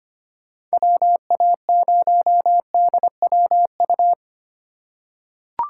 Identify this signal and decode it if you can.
Morse code, heard as a keyed tone, about 25 wpm, 700 Hz. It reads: WA0DWU
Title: USA Call Signs - 6 Characters - 3x Character Spacing - 25wpm